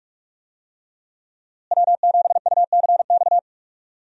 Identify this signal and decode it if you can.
Morse code, heard as a keyed tone, about 41 wpm, 700 Hz. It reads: W7UCX
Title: USA Call Signs - 5 Characters - Repeated 3x - 45wpm